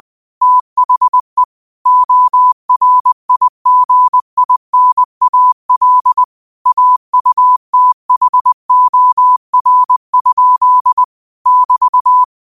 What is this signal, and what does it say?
Morse code, heard as a keyed tone, about 20 wpm, 1000 Hz. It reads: THE ORIGINAL AUTHOR? <BT>
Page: New quotes every day in morse code at 20 Words per minute.